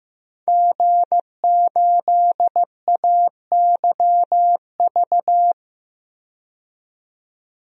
Morse code, heard as a keyed tone, about 15 wpm, 700 Hz. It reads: G8AYV
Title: DXCC-100 Call Signs - 5 Characters - 15wpm